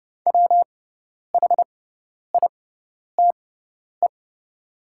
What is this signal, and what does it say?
Morse code, heard as a keyed tone, about 30 wpm, spaced as for 10 wpm, 700 Hz. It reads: WHITE
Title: 6x Character Spacing - Top 500 Words - 30wpm